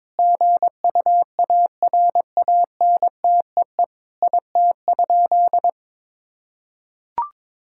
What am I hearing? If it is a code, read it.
Morse code, heard as a keyed tone, about 22 wpm, 700 Hz. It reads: GUARANTEE IT?
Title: Sentences from Top 1600 Words 22wpm